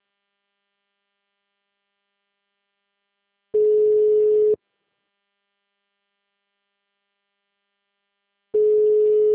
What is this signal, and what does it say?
Ringback tone (Europe, 425 Hz).
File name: out- -2043- - - .78315